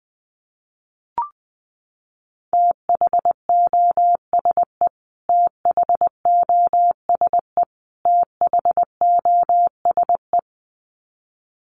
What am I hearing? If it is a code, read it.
Morse code, heard as a keyed tone, about 20 wpm, 700 Hz. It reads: THOSE THOSE THOSE
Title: Top 300 Words Repeated 20wpm